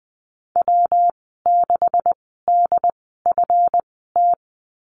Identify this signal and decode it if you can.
Morse code, heard as a keyed tone, about 20 wpm, spaced as for 14 wpm, 700 Hz. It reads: W6DFT
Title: 140 Lesson 23g - Letter f - Call Signs 20wpm